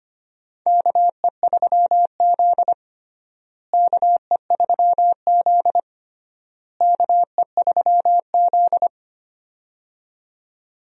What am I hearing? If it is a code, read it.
Morse code, heard as a keyed tone, about 25 wpm, 700 Hz. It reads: KE3Z KE3Z KE3Z
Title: USA Call Signs - 4 Characters - Repeated 3x - 25wpm